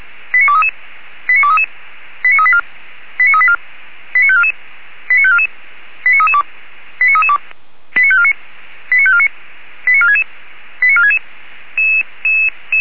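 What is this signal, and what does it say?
Five-tone selective call (ZVEI), heard as quick runs of five tones, tones of about 70 ms. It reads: 89233 89233 89384 89384 89544 89544 89332 89332 89549 89549 89466 89466
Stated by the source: ZVEI five-tone selective call 89288 received with an SDR, fire brigade Feuerwehr Orlamünde.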